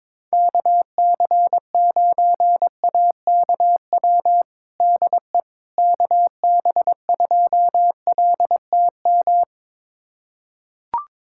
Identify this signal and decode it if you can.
Morse code, heard as a keyed tone, about 22 wpm, 700 Hz. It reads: KC9AKW DE KB2LTM
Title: USA Callsign DE Callsign 22wpm